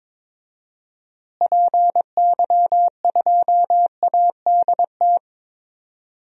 Morse code, heard as a keyed tone, about 22 wpm, 700 Hz. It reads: PY2ADT
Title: DXCC-100 Call Signs - 6 Characters - Repeated 2x - 22wpm